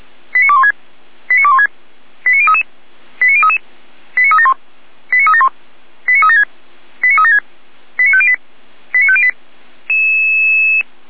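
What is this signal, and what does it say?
Five-tone selective call (ZVEI), heard as quick runs of five tones, tones of about 70 ms. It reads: 89216 89216 89033 89033 89371 89371 89376 89376 89508 89508
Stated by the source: ZVEI five-tone selective call received on SDR.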